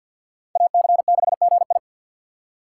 Morse code, heard as a keyed tone, about 47 wpm, 700 Hz. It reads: AC6GI
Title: USA Call Signs - 5 Characters - 3x Character Spacing - 50wpm